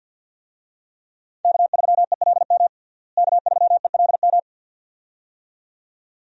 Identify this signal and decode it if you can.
Morse code, heard as a keyed tone, about 51 wpm, 700 Hz. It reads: K3ELM K3ELM
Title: USA Call Signs - 5 Characters - Repeated 2x - 50wpm